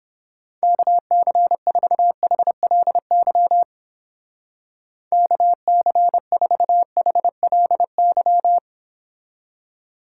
Morse code, heard as a keyed tone, about 29 wpm, 700 Hz. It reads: KC4HLY KC4HLY